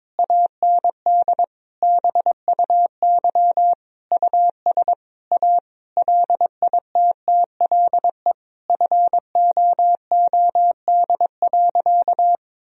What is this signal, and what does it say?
Morse code, heard as a keyed tone, about 22 wpm, 700 Hz. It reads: AND BUY US A LITTLE FOOD.
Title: Sentences from Top 1000 Words - Review Part 4 - 22wpm